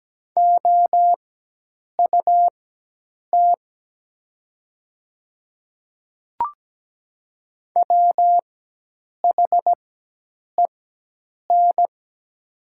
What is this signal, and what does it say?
Morse code, heard as a keyed tone, about 17 wpm, spaced as for 8 wpm, 700 Hz. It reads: OUT WHEN
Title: 4x Character Spacing - Top 100 Words - 17wpm